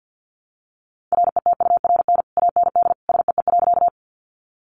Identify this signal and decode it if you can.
Morse code, heard as a keyed tone, about 50 wpm, 700 Hz. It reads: RETURN AND SEE.
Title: Sentences from Top 800 Words 50wpm